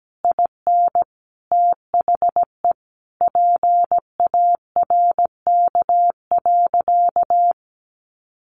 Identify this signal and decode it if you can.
Morse code, heard as a keyed tone, about 17 wpm, 700 Hz. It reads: IN THE PARK.
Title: Sentences from Top 1300 Words 17wpm